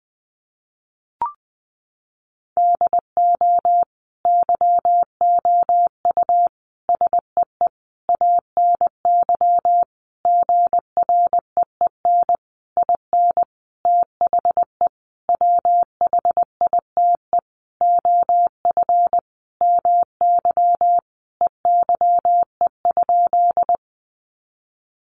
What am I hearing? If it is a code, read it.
Morse code, heard as a keyed tone, about 20 wpm, 700 Hz. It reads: DO YOU SEE ANY GREEN IN THE WHITE OF MY EYE?
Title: Sentences from Top 500 Words 20wpm